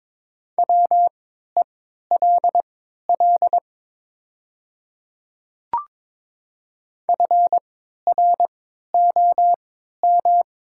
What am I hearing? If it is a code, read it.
Morse code, heard as a keyed tone, about 23 wpm, spaced as for 13 wpm, 700 Hz. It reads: WELL FROM
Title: 3x Character Spacing - Top 300 Words - 22wpm